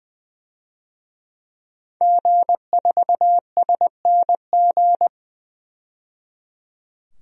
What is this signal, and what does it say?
Morse code, heard as a keyed tone, about 20 wpm, 700 Hz. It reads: G4SNG
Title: DXCC-100 Call Signs - 5 Characters - 20wpm